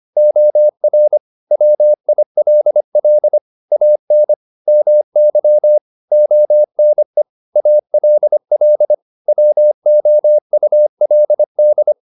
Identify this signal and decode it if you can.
Morse code, heard as a keyed tone, about 25 wpm, 600 Hz. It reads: OR WILL AN MY ONE ALL WOULD
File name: Top-100-Words-25